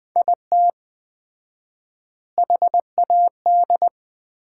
Morse code, heard as a keyed tone, about 20 wpm, 700 Hz. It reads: IT HAD
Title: 4x Word-Spacing - Sets of 3 Words - 20wpm